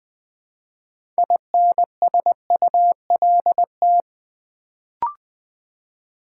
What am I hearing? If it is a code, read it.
Morse code, heard as a keyed tone, about 20 wpm, 700 Hz. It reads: INSULT
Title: Build A Word - 3 Letters At A Time - 6-Letter Words - 20wpm